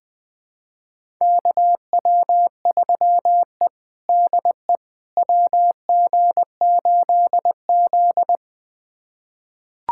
Morse code, heard as a keyed tone, about 20 wpm, 700 Hz. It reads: KW3E DE WG8Z
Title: USA Callsign DE Callsign - 4 Characters - 20wpm